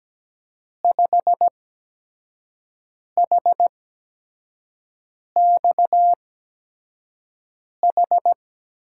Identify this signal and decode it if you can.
Morse code, heard as a keyed tone, about 17 wpm, spaced as for 5 wpm, 700 Hz. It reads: 5HXH
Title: USA Call Signs - 6 Characters - 8x Character Spacing - 17wpm